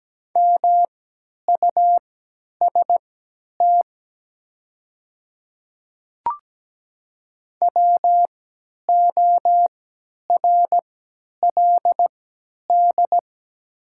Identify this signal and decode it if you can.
Morse code, heard as a keyed tone, about 17 wpm, spaced as for 10 wpm, 700 Hz. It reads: MUST WORLD
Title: 3x Character Spacing - Top 500 Words - 17wpm